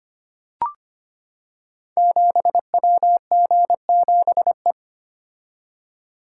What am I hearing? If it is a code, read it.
Morse code, heard as a keyed tone, about 25 wpm, 700 Hz. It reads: 7WG7E
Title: Code Groups - 5 Letter:Numbers - 25wpm